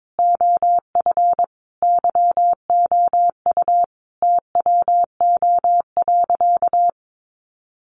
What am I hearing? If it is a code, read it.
Morse code, heard as a keyed tone, about 22 wpm, 700 Hz. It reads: OF YOU TWO.
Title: Sentences from Top 700 Words 22wpm